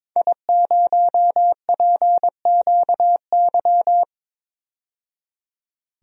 Morse code, heard as a keyed tone, about 22 wpm, 700 Hz. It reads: I0PQY
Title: DXCC-100 Call Signs - 5 Characters - 22wpm